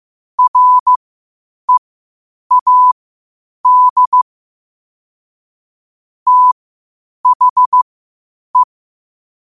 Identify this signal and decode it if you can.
Morse code, heard as a keyed tone, about 15 wpm, spaced as for 8 wpm, 1000 Hz. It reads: READ THE
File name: qotdfiles / 8wpm